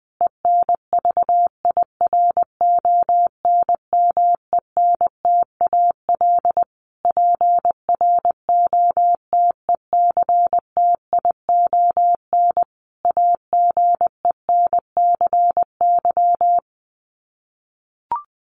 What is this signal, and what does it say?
Morse code, heard as a keyed tone, about 20 wpm, 700 Hz. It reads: ENVIRONMENTAL PROTECTION AGENCY
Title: Sets of 5 Words Encore2 20wpm